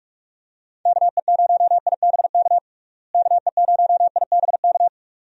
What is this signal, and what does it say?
Morse code, heard as a keyed tone, about 46 wpm, 700 Hz. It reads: KE0IBK KE0IBK
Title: USA Call Signs - 6 Characters - Repeated 2x - 45wpm